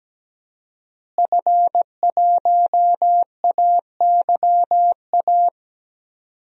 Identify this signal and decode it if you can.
Morse code, heard as a keyed tone, about 17 wpm, 700 Hz. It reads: F1AYA
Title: DXCC-100 Call Signs - 5 Characters - Repeated 2x - 17wpm